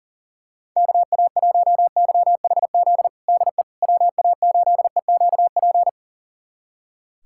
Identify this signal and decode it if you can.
Morse code, heard as a keyed tone, about 37 wpm, 700 Hz. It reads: KA1YHZ DE WA8EQP
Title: USA Callsign DE Callsign - 6 Characters - 40wpm